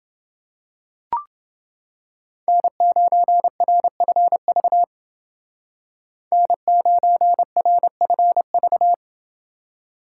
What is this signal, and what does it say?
Morse code, heard as a keyed tone, about 28 wpm, 700 Hz. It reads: N9RFV N9RFV